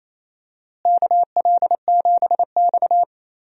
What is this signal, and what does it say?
Morse code, heard as a keyed tone, about 28 wpm, 700 Hz. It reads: KL7X
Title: DXCC-100 Call Signs - 4 Characters - Repeated 2x - 28wpm